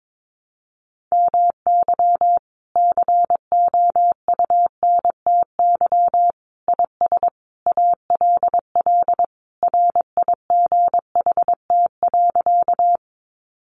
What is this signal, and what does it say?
Morse code, heard as a keyed tone, about 22 wpm, 700 Hz. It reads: MY COUNTY IS ALL RIGHT.
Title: Sentences from Top 1200 Words 22wpm